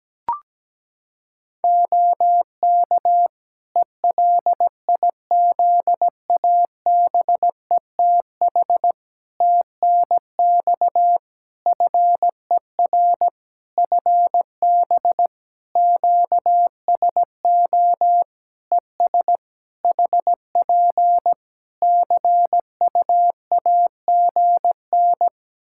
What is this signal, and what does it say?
Morse code, heard as a keyed tone, about 17 wpm, 700 Hz. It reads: OK ELIZABETH TNX FER FB QSO ES HP CUAGN
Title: Practice QSO - Exchange 4 - Ending - 17wpm